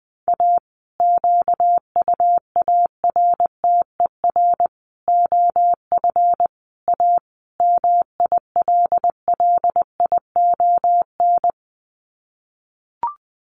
Morse code, heard as a keyed tone, about 20 wpm, 700 Hz. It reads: A QUARTER OF A MILLION
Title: Sets of 5 Words Encore2 20wpm